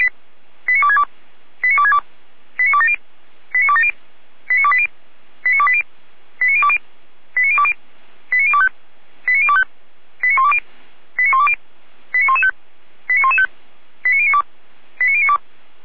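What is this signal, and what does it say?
Five-tone selective call (ZVEI), heard as quick runs of five tones, tones of about 70 ms. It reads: 89262 89262 89270 89270 89280 89280 89020 89020 89025 89025 89120 89120 89115 89115 89092 89092
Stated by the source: Five-tone ZVEI selective call received on SDR, 455 path.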